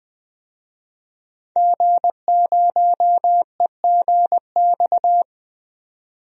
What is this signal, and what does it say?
Morse code, heard as a keyed tone, about 20 wpm, 700 Hz. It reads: G0EGX
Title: DXCC-100 Call Signs - 5 Characters - 20wpm